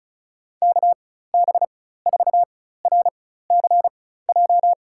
Morse code, heard as a keyed tone, about 36 wpm, spaced as for 16 wpm, 700 Hz. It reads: KB4RCJ
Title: USA Call Signs - 4x Character Spacing - 35wpm